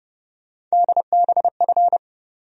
Morse code, heard as a keyed tone, about 30 wpm, 700 Hz. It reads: DBF
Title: USA Call Sign Postfixes - 3 Letters 30wpm